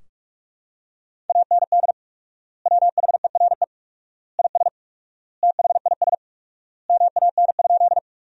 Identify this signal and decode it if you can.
Morse code, heard as a keyed tone, about 47 wpm, 700 Hz. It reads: AND WHERE IS THIS MAN?
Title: 4x Word-Spacing - Sentences from Top 200 Words - 45wpm